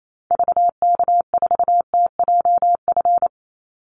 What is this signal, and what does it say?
Morse code, heard as a keyed tone, about 28 wpm, 700 Hz. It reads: VK4TJF
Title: Call Signs - 28wpm